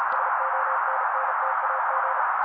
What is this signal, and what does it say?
Morse code, heard as a keyed tone, about 35 wpm, 530 Hz. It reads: SONNTAG
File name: daily / 35wpm noise